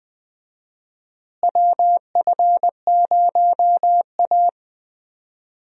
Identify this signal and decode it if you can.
Morse code, heard as a keyed tone, about 20 wpm, 700 Hz. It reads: WF0A
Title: USA Call Signs - 4 Characters - Repeated 2x - 20wpm